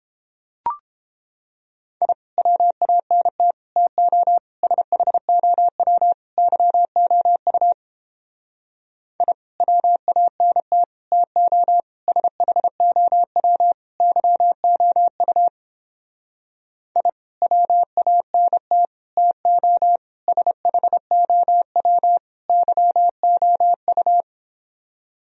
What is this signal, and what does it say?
Morse code, heard as a keyed tone, about 29 wpm, 700 Hz. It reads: I WANT TO SHOW YOU I WANT TO SHOW YOU I WANT TO SHOW YOU